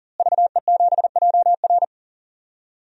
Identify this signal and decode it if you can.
Morse code, heard as a keyed tone, about 40 wpm, 700 Hz. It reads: VE7JR
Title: DXCC-100 Call Signs - 5 Characters - Repeated 2x - 40wpm